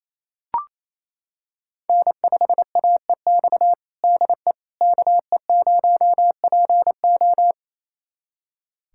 Morse code, heard as a keyed tone, about 28 wpm, 700 Hz. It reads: N5AEX DE KE0PO
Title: USA Callsign DE Callsign - 5 Characters - 28wpm